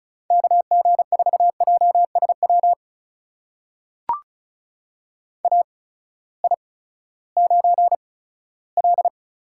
Morse code, heard as a keyed tone, about 36 wpm, spaced as for 18 wpm, 700 Hz. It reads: KG4JSW AI9L
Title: USA Call Signs - 8x Character Spacing - 35wpm